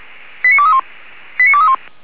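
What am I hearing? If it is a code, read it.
Five-tone selective call (ZVEI), heard as quick runs of five tones, tones of about 70 ms. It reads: 89231 89231
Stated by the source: Five-tone ZVEI selective call 89241 received by SDR.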